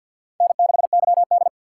Morse code, heard as a keyed tone, about 60 wpm, 700 Hz. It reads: N6YD
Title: Speed Racing - USA 1x2 Call Signs 50wpm